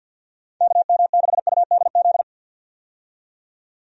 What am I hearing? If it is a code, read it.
Morse code, heard as a keyed tone, about 51 wpm, 700 Hz. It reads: KM6UDZ
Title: USA Call Signs - 6 Characters - 50wpm